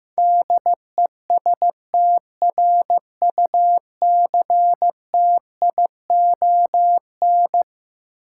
Morse code, heard as a keyed tone, about 15 wpm, 700 Hz. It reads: DESTRUCTION
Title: Sets of 5 Words Encore - New Word Drill - 15wpm